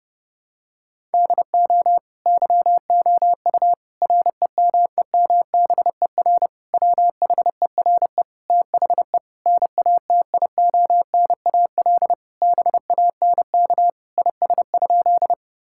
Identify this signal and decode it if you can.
Morse code, heard as a keyed tone, about 30 wpm, 700 Hz. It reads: DO YOU REMEMBER WHERE THE NATIONAL BANK IS?
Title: Sentences from Top 1000 Words - Review Part 6 - 30wpm